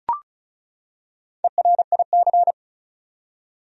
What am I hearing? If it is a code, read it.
Morse code, heard as a keyed tone, about 36 wpm, 700 Hz. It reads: ERIC
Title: USA Common Names - Max 4 Letters - 35wpm